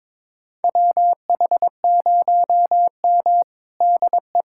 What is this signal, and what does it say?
Morse code, heard as a keyed tone, about 22 wpm, 700 Hz. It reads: WH0M DE
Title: USA Callsign DE Callsign - 4 Characters - 22wpm